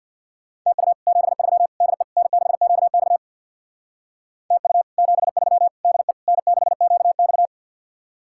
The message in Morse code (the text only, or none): TU 73 DE N6QX TU 73 DE N6QX